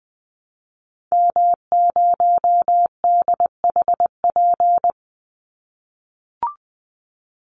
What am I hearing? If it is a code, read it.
Morse code, heard as a keyed tone, about 20 wpm, 700 Hz. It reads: M0DHP
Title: Call Signs - 20wpm